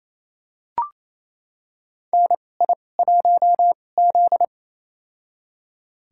Morse code, heard as a keyed tone, about 28 wpm, spaced as for 20 wpm, 700 Hz. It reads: NI1Z
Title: USA Call Signs - 4 Characters - 2x Character Spacing - 28wpm